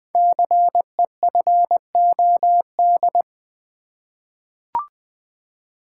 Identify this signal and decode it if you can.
Morse code, heard as a keyed tone, about 20 wpm, 700 Hz. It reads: CEFOD
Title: Code Groups - 5 Letters - 20wpm